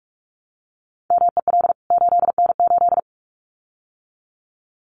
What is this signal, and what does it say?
Morse code, heard as a keyed tone, about 44 wpm, 700 Hz. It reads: MEL 8N8
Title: CWT Exchange 45wpm